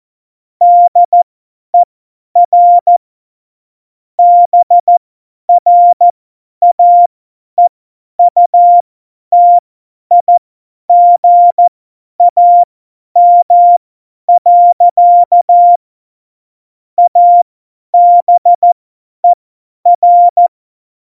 Morse code, heard as a keyed tone, about 14 wpm, spaced as for 10 wpm, 700 Hz. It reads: DER BRAEUTIGAM. ABER
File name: Morsekurs / Witze / Witz 24.Dezember 2019, 50 F70